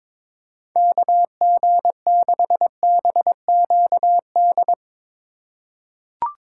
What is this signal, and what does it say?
Morse code, heard as a keyed tone, about 23 wpm, 700 Hz. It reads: KG6BQD
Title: USA Call Signs - 4x Character Spacing - 22wpm